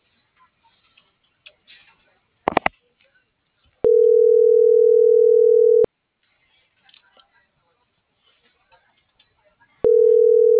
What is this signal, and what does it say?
Ringback tone (North America), 440 + 480 Hz.